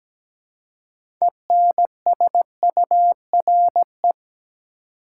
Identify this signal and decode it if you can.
Morse code, heard as a keyed tone, about 17 wpm, 700 Hz. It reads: ENSURE
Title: Six Letter Words - 17wpm